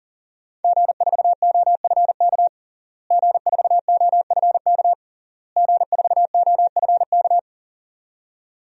Morse code, heard as a keyed tone, about 40 wpm, 700 Hz. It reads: G4OFK G4OFK G4OFK
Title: DXCC-100 Call Signs - 5 Characters - Repeated 3x - 40wpm